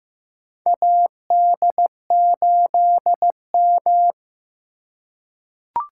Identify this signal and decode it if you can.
Morse code, heard as a keyed tone, about 15 wpm, 700 Hz. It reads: AD8M
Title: USA Call Signs - 4x Character Spacing - 15wpm